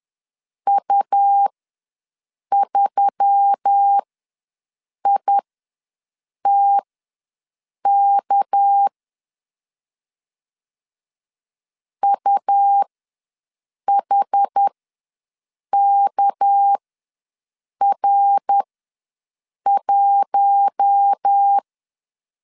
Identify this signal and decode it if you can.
Morse code, heard as a keyed tone, about 10 wpm, spaced as for 6 wpm, 800 Hz. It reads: U3ITK UHKR1
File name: LEZIONI / 6 PKX0 / 6 PKX ws 10wpm 3706